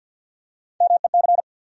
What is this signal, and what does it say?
Morse code, heard as a keyed tone, about 50 wpm, 700 Hz. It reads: MEC
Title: USA Call Sign Postfixes - 3 Letters 50wpm